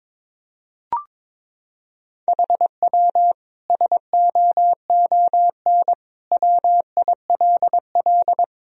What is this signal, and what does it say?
Morse code, heard as a keyed tone, about 21 wpm, 700 Hz. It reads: HW SOON WILL
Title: Sentences from Top 400 Words and Abbreviations 22wpm